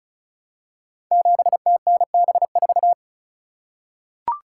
Morse code, heard as a keyed tone, about 36 wpm, 700 Hz. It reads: 7TNB4
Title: Code Groups - 5 Letter:Numbers - 35wpm